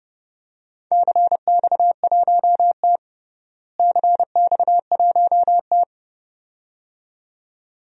Morse code, heard as a keyed tone, about 32 wpm, 700 Hz. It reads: CX1T CX1T
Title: DXCC-100 Call Signs - 4 Characters - Repeated 2x - 30wpm